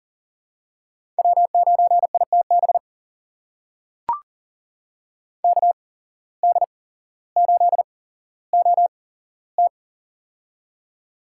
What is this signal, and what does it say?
Morse code, heard as a keyed tone, about 40 wpm, spaced as for 17 wpm, 700 Hz. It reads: W9ITB KD8OT
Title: USA Call Signs - 5 Characters - 8x Character Spacing - 40wpm